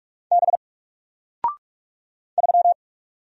Morse code, heard as a keyed tone, about 43 wpm, 700 Hz. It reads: B 3
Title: Single Character (Letter:Number:Punctuation:Prosign) Alt 45wpm